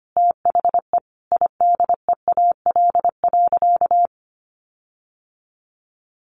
Morse code, heard as a keyed tone, about 25 wpm, 700 Hz. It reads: THE IDEAL.
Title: Sentences from Top 1400 Words 25wpm